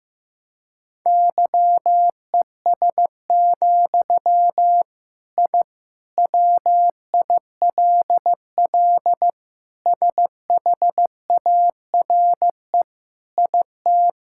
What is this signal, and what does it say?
Morse code, heard as a keyed tone, about 16 wpm, 700 Hz. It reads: YES, I WILL SHARE IT
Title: Sentences from Top 1000 Words - Review Part 6 - 15wpm